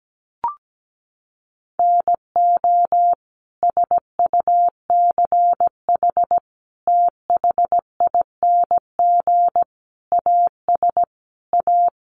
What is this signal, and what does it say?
Morse code, heard as a keyed tone, about 17 wpm, 700 Hz. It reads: NO SUCH THING AS A
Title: Sets of 5 Words Encore 17wpm